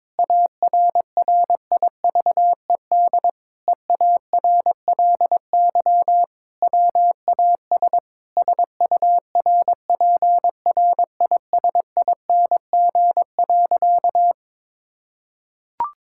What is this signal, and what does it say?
Morse code, heard as a keyed tone, about 22 wpm, 700 Hz. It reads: ARRIVED EARLY WAS SURPRISING.
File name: practice / 22wpm / Sentences with 4-Word Phrases - 22wpm